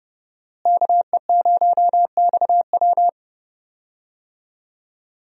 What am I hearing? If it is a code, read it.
Morse code, heard as a keyed tone, about 29 wpm, 700 Hz. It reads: KE0XW
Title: USA Call Signs - 5 Characters - Repeated 2x - 30wpm